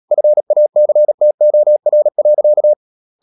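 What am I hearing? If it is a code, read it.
Morse code, heard as a keyed tone, about 37 wpm, 600 Hz. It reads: FACTOR.
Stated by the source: Morse code practice at 35–39 wpm sending English text of Ken Follett's novel Never, from CW med Gnister.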